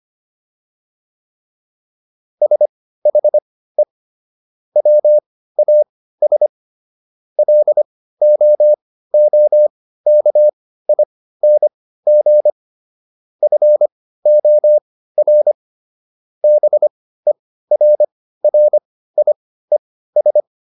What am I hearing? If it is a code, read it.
Morse code, heard as a keyed tone, about 25 wpm, spaced as for 15 wpm, 600 Hz. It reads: SHE WAS LOOKING FOR BERRIES